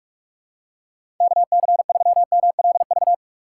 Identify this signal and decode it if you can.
Morse code, heard as a keyed tone, about 46 wpm, 700 Hz. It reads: KC3MLV
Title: USA Call Signs - 6 Characters - 2x Character Spacing - 45wpm